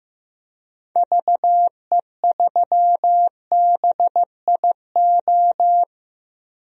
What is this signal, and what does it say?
Morse code, heard as a keyed tone, about 15 wpm, 700 Hz. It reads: VE3BIO
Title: DXCC-100 Call Signs - 6 Characters - 15wpm